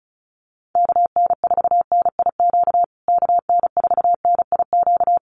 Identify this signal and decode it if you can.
Morse code, heard as a keyed tone, about 34 wpm, 700 Hz. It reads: KN4NIQ KN4NIQ
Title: USA Call Signs Repeated 3x 35wpm